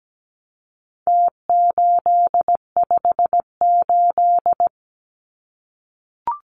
Morse code, heard as a keyed tone, about 17 wpm, 700 Hz. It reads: T858
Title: Numbers - Contest Style - 4-Digit 17wpm